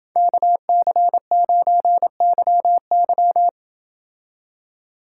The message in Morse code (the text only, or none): KC9YY